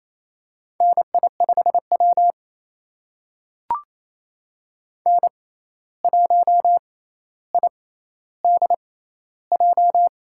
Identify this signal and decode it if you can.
Morse code, heard as a keyed tone, about 28 wpm, spaced as for 13 wpm, 700 Hz. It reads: NI5W N1IDJ
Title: USA Call Signs - 6x Character Spacing - 28wpm